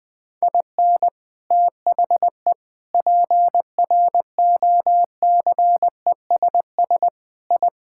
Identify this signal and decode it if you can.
Morse code, heard as a keyed tone, about 20 wpm, 700 Hz. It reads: IN THE PROCESS I
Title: Sentences with 3-Word Phrases - 20wpm